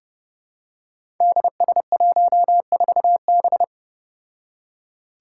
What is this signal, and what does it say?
Morse code, heard as a keyed tone, about 30 wpm, 700 Hz. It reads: DS14B
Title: Five Challenging Letter-Number Combinations 30wpm